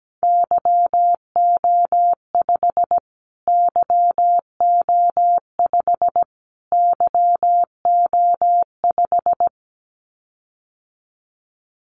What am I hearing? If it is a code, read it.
Morse code, heard as a keyed tone, about 17 wpm, 700 Hz. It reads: YO5 YO5 YO5
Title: Callsign Prefixes for Top 25 DXCC - Repeated 3x - 17wpm